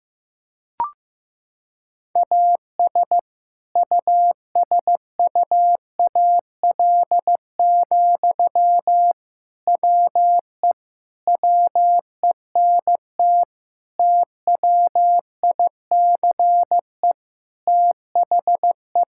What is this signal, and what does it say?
Morse code, heard as a keyed tone, about 16 wpm, 700 Hz. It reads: AS USUAL, WE WENT TWICE THE
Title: Sentences from Top 1600 Words 15wpm